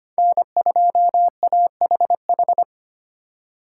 Morse code, heard as a keyed tone, about 25 wpm, 700 Hz. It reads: N2AHH
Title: USA Call Signs - 5 Characters - 4x Character Spacing - 25wpm